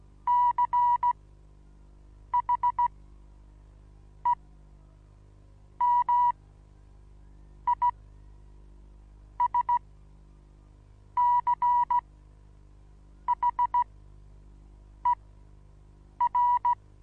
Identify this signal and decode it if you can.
Morse code, heard as a keyed tone, about 16 wpm, spaced as for 6 wpm, 970 Hz. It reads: CHEMISCHER
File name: Onlinelehrgang / morsen / DJ4UF Morsen25